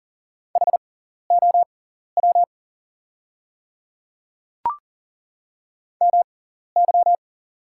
Morse code, heard as a keyed tone, about 40 wpm, spaced as for 14 wpm, 690 Hz. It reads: HOW MY
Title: 6x Character Spacing - Top 100 Words - 40wpm